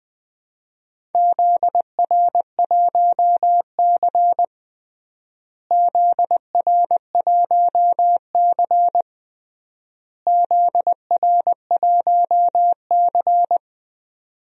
Morse code, heard as a keyed tone, about 20 wpm, 700 Hz. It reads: ZR1C ZR1C ZR1C
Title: DXCC-100 Call Signs - 4 Characters - Repeated 3x - 20wpm